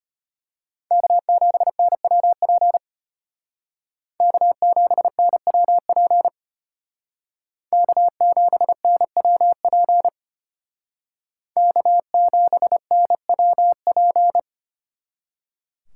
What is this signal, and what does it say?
Morse code, heard as a keyed tone, about 32 wpm, 700 Hz. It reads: K7NWP K7NWP K7NWP K7NWP